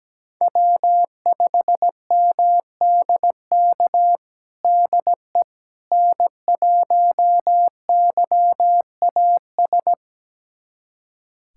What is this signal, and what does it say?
Morse code, heard as a keyed tone, about 17 wpm, 700 Hz. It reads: W5MDK DE N1YAS
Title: USA Callsign DE Callsign - 5 Characters - 17wpm